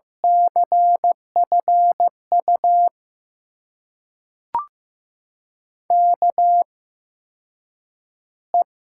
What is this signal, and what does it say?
Morse code, heard as a keyed tone, about 15 wpm, spaced as for 8 wpm, 700 Hz. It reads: CFU KE
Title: USA Call Signs - 6 Characters - 8x Character Spacing - 15wpm